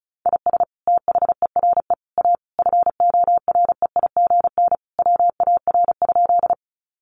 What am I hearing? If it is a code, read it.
Morse code, heard as a keyed tone, about 34 wpm, 700 Hz. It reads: IS THERE A FOREIGN WAR?
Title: Sentences from Top 1400 Words 35wpm